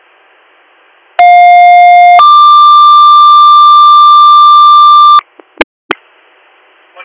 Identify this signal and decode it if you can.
Two-tone page, heard as a one-second tone, then a three-second tone, about 710 then 1180 Hz. It is heard from 1.2 to 5.2 s.